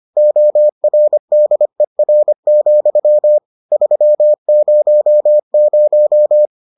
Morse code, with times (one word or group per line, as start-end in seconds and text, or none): 0.2-3.4 ORDER,
3.7-6.5 300